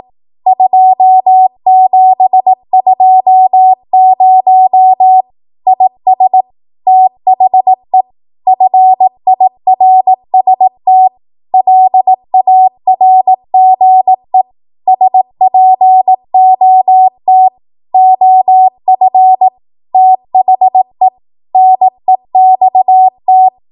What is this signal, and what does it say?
Morse code, heard as a keyed tone, about 18 wpm, 750 Hz. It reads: 2720 IS THE FIRST LARGE SPOT OF THE NEXT
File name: Archive / 18 WPM / 18WPM